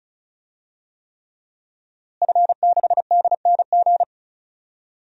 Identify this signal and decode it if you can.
Morse code, heard as a keyed tone, about 36 wpm, 700 Hz. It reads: F6DNG
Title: DXCC-100 Call Signs - 5 Characters - Repeated 3x - 35wpm